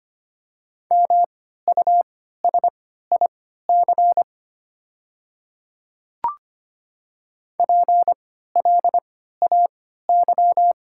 Morse code, heard as a keyed tone, about 25 wpm, spaced as for 14 wpm, 700 Hz. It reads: MUSIC PLAY
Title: 3x Character Spacing - Top 300 Words - 25wpm